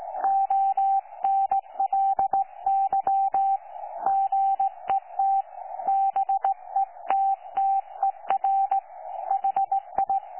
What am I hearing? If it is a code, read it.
Morse code, heard as a keyed tone, about 17 wpm, 780 Hz. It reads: ONLY GET BETTER HI
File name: 12 39Z 18088.2kHz utah aa7fv k9tjl